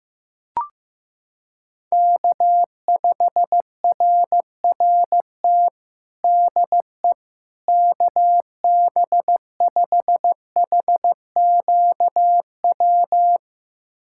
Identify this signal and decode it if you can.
Morse code, heard as a keyed tone, about 15 wpm, 700 Hz. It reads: K5RRT DE KB5HQW
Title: USA Callsign DE Callsign 15wpm